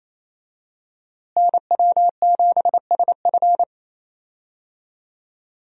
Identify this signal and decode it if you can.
Morse code, heard as a keyed tone, about 28 wpm, 700 Hz. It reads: NW7SF
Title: USA Call Signs - 5 Characters - Repeated 3x - 28wpm